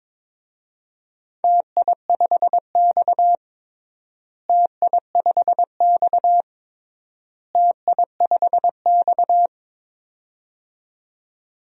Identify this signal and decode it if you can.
Morse code, heard as a keyed tone, about 22 wpm, 700 Hz. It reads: TI5X TI5X TI5X
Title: DXCC-100 Call Signs - 4 Characters - Repeated 3x - 22wpm